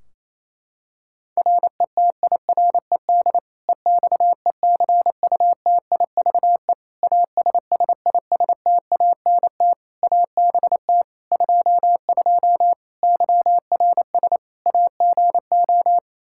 Morse code, heard as a keyed tone, about 28 wpm, 700 Hz. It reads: RETIRED EXECUTIVE ASSISTANT ABT 22 YRS AGO
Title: Practice QSO - Exchange 3 - More Info - 28wpm